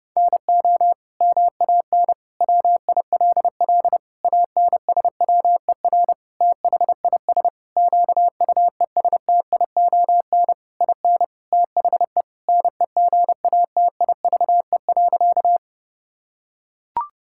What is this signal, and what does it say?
Morse code, heard as a keyed tone, about 32 wpm, 700 Hz. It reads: NO MAN WILL ANSWER THIS QUESTION IN THE NEGATIVE.
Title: Sentences from Top 1000 Words - Review Part 6 - 30wpm